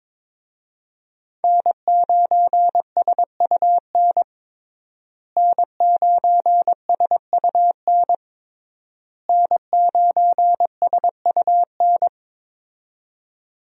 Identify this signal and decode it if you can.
Morse code, heard as a keyed tone, about 22 wpm, 700 Hz. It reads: N9SUN N9SUN N9SUN
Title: USA Call Signs - 5 Characters - Repeated 3x - 22wpm